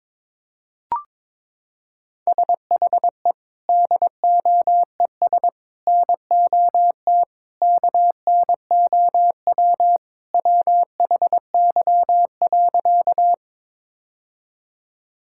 Morse code, heard as a keyed tone, about 22 wpm, 700 Hz. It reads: SHE DOES NOT KNOW WHY.
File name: practice / 22wpm / Sentences with 4-Word Phrases - 22wpm